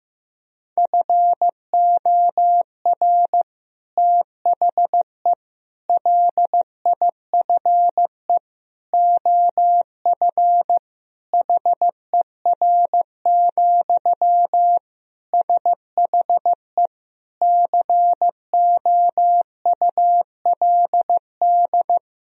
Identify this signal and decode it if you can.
Morse code, heard as a keyed tone, about 15 wpm, 700 Hz. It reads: FOR THE LIFE OF HER, SHE COULD
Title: Sentences from Top 1000 Words - Review Part 2 - 15wpm